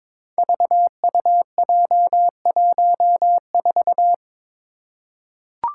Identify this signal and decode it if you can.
Morse code, heard as a keyed tone, about 22 wpm, 700 Hz. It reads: VUJ14
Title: Five Challenging Letter-Number Combinations 22wpm